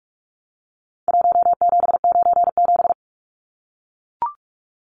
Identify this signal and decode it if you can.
Morse code, heard as a keyed tone, about 44 wpm, 700 Hz. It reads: 1797
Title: Numbers - Contest Style - 4-Digit 45wpm